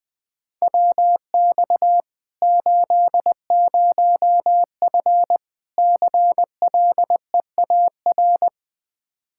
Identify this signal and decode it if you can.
Morse code, heard as a keyed tone, about 21 wpm, 700 Hz. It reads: WX 80F CLEAR
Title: Practice QSO - Exchange 2 - Info - 20wpm